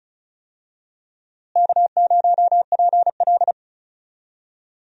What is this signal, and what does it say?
Morse code, heard as a keyed tone, about 36 wpm, 700 Hz. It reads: K0PL
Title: USA Call Signs - 4 Characters - Repeated 3x - 35wpm